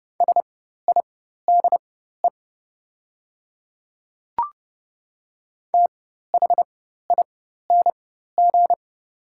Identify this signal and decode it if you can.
Morse code, heard as a keyed tone, about 30 wpm, spaced as for 14 wpm, 700 Hz. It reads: SIDE THING
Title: 4x Character Spacing - Top 100 Words - 30wpm